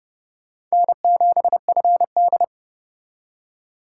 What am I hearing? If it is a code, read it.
Morse code, heard as a keyed tone, about 36 wpm, 700 Hz. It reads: N7FD
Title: Speed Racing - USA 1x2 Call Signs 30wpm